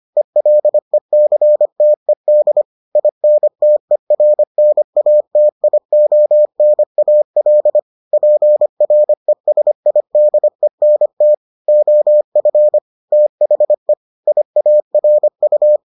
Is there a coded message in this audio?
Yes: Morse code from 0.2 to 15.8 s, heard as a keyed tone, about 25 wpm, 600 Hz. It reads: ELECTED INTERNATIONAL PRESIDENT OF THE IARU